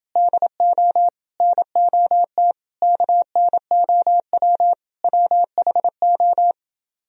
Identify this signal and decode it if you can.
Morse code, heard as a keyed tone, about 30 wpm, 700 Hz. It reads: DO NOT KNOW WHO